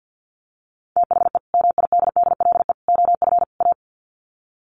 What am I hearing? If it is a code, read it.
Morse code, heard as a keyed tone, about 50 wpm, 700 Hz. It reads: THE MIDDLE OF A
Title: Sets of 4 Words 50wpm